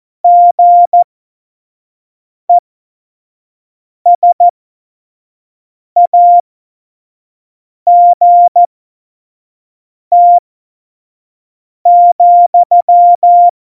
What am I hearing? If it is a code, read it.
Morse code, heard as a keyed tone, about 14 wpm, spaced as for 5 wpm, 700 Hz. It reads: GESAGT,
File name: Morsekurs / Witze / Witz 15. Dezember, F70